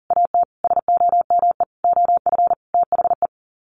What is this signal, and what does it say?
Morse code, heard as a keyed tone, about 40 wpm, 700 Hz. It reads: AT SOME OF THE
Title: Sets of 5 Words Encore 40wpm